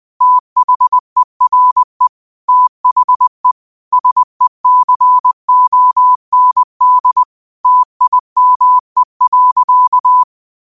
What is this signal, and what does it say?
Morse code, heard as a keyed tone, about 20 wpm, 1000 Hz. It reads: THERE THE SECOND TIME.
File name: qotdfiles / 20wpm